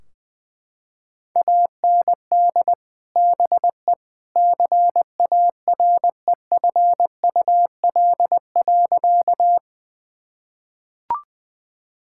Morse code, heard as a keyed tone, about 20 wpm, 700 Hz. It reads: AND BE CAREFUL.
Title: Sentences from Top 1600 Words 20wpm